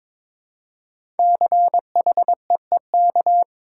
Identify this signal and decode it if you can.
Morse code, heard as a keyed tone, about 22 wpm, 700 Hz. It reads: CHEEK
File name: 5 Letter Words - First 3 Letters Given - 22wpm